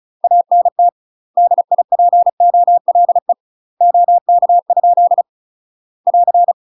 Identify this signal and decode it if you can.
Morse code, heard as a keyed tone, about 34 wpm, 700 Hz. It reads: ANT DIPOLE OK? <AR>
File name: UK MASTER QSOs TRAINING / 35WPM / UK MASTER QSOs 0004